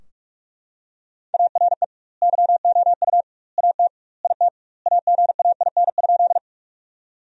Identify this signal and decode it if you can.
Morse code, heard as a keyed tone, about 44 wpm, 700 Hz. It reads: ARE YOU AT IT AGAIN?
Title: 2x Word-Spacing - Sentences from Top 200 Words - 45wpm